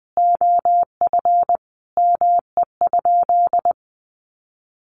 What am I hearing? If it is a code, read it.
Morse code, heard as a keyed tone, about 20 wpm, 700 Hz. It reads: OF ME?
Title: Sentences from Top 1100 Words 20wpm